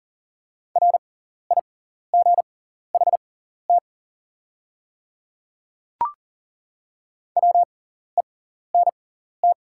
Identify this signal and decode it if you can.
Morse code, heard as a keyed tone, about 40 wpm, spaced as for 14 wpm, 700 Hz. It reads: RIGHT WENT
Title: 6x Character Spacing - Top 200 Words - 40wpm